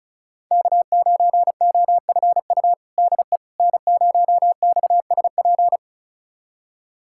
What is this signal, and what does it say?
Morse code, heard as a keyed tone, about 36 wpm, 690 Hz. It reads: K9OFU DE N0XSP
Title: USA Callsign DE Callsign - 5 Characters - 35wpm